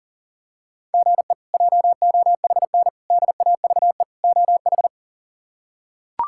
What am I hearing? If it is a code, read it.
Morse code, heard as a keyed tone, about 40 wpm, 700 Hz. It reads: GE JOHN DAVE OH
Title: K1USN SST QSO Practice - 40wpm